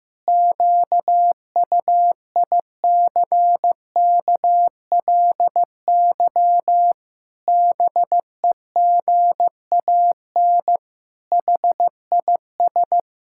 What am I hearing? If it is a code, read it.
Morse code, heard as a keyed tone, about 16 wpm, 700 Hz. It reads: QUICKLY BEGAN HIS
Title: Pangram Challenge v2 - 15wpm